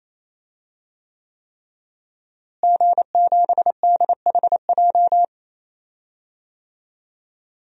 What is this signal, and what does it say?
Morse code, heard as a keyed tone, about 28 wpm, 700 Hz. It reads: G7DHJ
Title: DXCC-100 Call Signs - 5 Characters - Repeated 3x - 28wpm